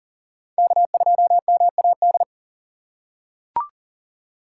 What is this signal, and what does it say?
Morse code, heard as a keyed tone, about 40 wpm, 700 Hz. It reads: K2MAD
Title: USA Call Signs - 5 Characters - 4x Character Spacing - 40wpm